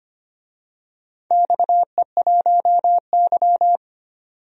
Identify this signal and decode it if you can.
Morse code, heard as a keyed tone, about 25 wpm, 700 Hz. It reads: XE1Y
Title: DXCC-100 Call Signs - 4 Characters - 25wpm